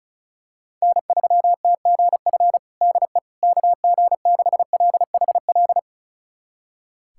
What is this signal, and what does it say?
Morse code, heard as a keyed tone, about 34 wpm, 700 Hz. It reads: N3TGF DE KG6LHL
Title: USA Callsign DE Callsign 35wpm